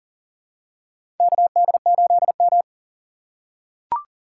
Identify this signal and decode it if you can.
Morse code, heard as a keyed tone, about 38 wpm, 700 Hz. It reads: KD8M
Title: USA Call Signs - 4 Characters - 2x Character Spacing - 40wpm